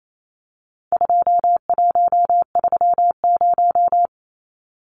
Morse code, heard as a keyed tone, about 28 wpm, 700 Hz. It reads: 2130
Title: Numbers - 4-Digit 28wpm